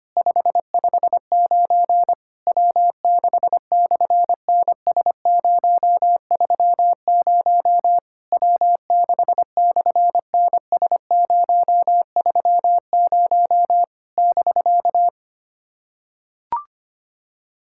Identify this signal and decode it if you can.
Morse code, heard as a keyed tone, about 25 wpm, 700 Hz. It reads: 559 W6/NS030 W6/NS030 <BK>
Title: SOTA QSO Practice 25wpm